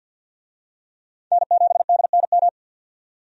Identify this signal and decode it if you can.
Morse code, heard as a keyed tone, about 49 wpm, 700 Hz. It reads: N7DNM
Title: USA Call Signs - 5 Characters - 50wpm